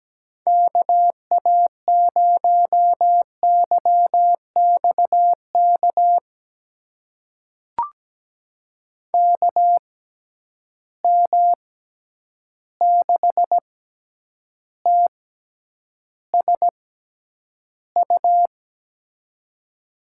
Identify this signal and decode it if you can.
Morse code, heard as a keyed tone, about 17 wpm, spaced as for 9 wpm, 700 Hz. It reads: KA0YXK KM6TSU
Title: USA Call Signs - 6 Characters - 6x Character Spacing - 17wpm